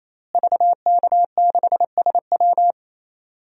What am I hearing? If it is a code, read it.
Morse code, heard as a keyed tone, about 28 wpm, 700 Hz. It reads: VK6SW
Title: DXCC-100 Call Signs - 5 Characters - Repeated 3x - 28wpm